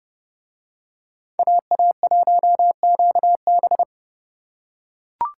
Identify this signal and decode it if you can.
Morse code, heard as a keyed tone, about 29 wpm, 700 Hz. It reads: AA1QB